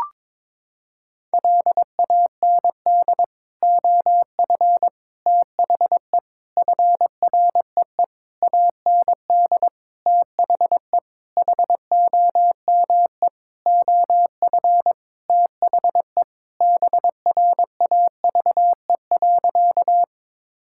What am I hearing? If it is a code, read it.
Morse code, heard as a keyed tone, about 22 wpm, 700 Hz. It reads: LAND OF THE FREE AND THE HOME OF THE BRAVE.
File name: Sentences with 9-Word Phrases - 22wpm